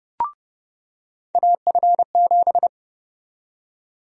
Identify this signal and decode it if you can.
Morse code, heard as a keyed tone, about 30 wpm, 700 Hz. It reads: AF7
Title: USA Call Sign Prefixes - 30wpm